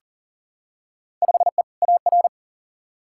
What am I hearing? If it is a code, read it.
Morse code, heard as a keyed tone, about 40 wpm, 700 Hz. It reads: 5E AR
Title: Field Day Exchanges Encore - 40wpm